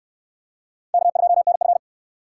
Morse code, heard as a keyed tone, about 69 wpm, 700 Hz. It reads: K2NF